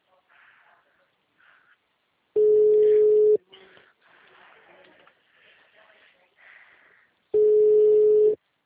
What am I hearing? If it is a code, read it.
Ringback tone (Europe, 425 Hz).
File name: out- -2013- - - .92